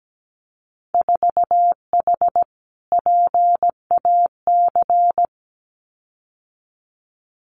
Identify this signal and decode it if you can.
Morse code, heard as a keyed tone, about 17 wpm, 700 Hz. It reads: 4H PAC
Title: Winter Field Day Exchanges Encore 17wpm